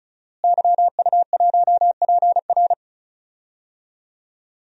Morse code, heard as a keyed tone, about 34 wpm, 700 Hz. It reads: YU1PR
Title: DXCC-100 Call Signs - 5 Characters - Repeated 2x - 35wpm